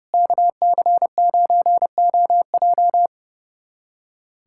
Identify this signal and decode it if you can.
Morse code, heard as a keyed tone, about 30 wpm, 700 Hz. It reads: KC9OJ
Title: USA Callsign DE Callsign - 5 Characters - 30wpm